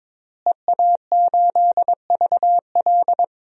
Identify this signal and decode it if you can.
Morse code, heard as a keyed tone, about 22 wpm, 700 Hz. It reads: EA8VL
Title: DXCC-100 Call Signs - 5 Characters - Repeated 3x - 22wpm